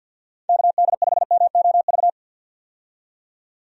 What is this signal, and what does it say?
Morse code, heard as a keyed tone, about 51 wpm, 700 Hz. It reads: KD5MOV
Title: USA Call Signs - 6 Characters - 50wpm